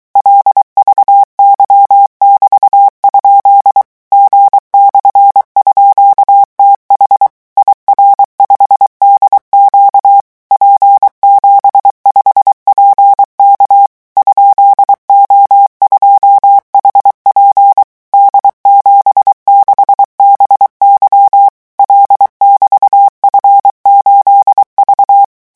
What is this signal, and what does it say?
Morse code, heard as a keyed tone, about 23 wpm, 800 Hz. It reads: LVY<BT>? G/_TH IR5DQ P75PK ?O2HP D76BY L<BT>F8V
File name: morse mp3 / revision13-vitesse 22 mots